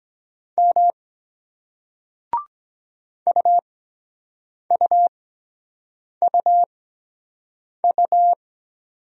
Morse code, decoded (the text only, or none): M UUUU